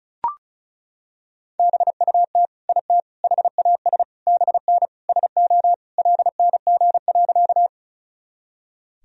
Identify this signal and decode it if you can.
Morse code, heard as a keyed tone, about 34 wpm, 700 Hz. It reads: BUT IT HAS BN SO LNG.
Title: Sentences from Top 100 Words and Abbreviations 35wpm